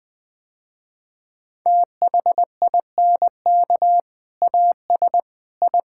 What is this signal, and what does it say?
Morse code, heard as a keyed tone, about 20 wpm, 700 Hz. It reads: THINK AS I
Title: Sentences with 2-Word Phrases - Spoken Beginning - 20wpm